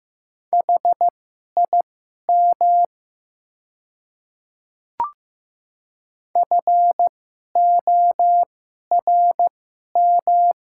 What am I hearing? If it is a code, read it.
Morse code, heard as a keyed tone, about 15 wpm, spaced as for 11 wpm, 700 Hz. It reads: HIM FORM
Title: 2x Character Spacing - Top 200 Words - 15wpm